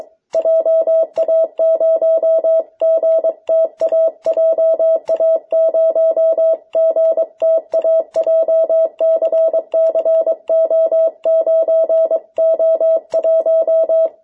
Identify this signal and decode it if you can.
Morse code, heard as a keyed tone, about 22 wpm, 650 Hz. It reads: JA0GTA JA0GTA JCCO9O1
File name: jh8cbh / 2021cw